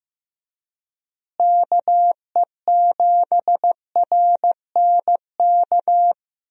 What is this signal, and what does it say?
Morse code, heard as a keyed tone, about 15 wpm, 700 Hz. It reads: KE7RNK
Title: USA Call Signs - 6 Characters - Repeated 2x - 15wpm